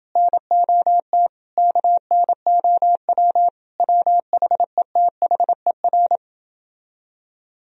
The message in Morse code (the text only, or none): NOT KNOW WHETHER